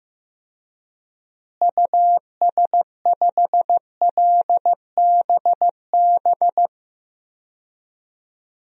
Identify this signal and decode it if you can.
Morse code, heard as a keyed tone, about 16 wpm, 700 Hz. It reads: US5LBB
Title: DXCC-100 Call Signs - 6 Characters - Repeated 2x - 15wpm